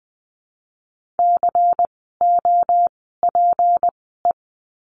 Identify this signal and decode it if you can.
Morse code, heard as a keyed tone, about 20 wpm, spaced as for 14 wpm, 700 Hz. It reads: COPE
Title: 153 Lesson 25f - Letter p - Words 20wpm